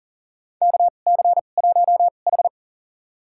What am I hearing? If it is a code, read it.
Morse code, heard as a keyed tone, about 40 wpm, spaced as for 29 wpm, 690 Hz. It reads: KC1H
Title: USA Call Signs - 4 Characters - 2x Character Spacing - 40wpm